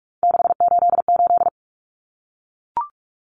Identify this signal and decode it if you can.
Morse code, heard as a keyed tone, about 44 wpm, 700 Hz. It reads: 688
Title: Numbers - 3-Digit 45wpm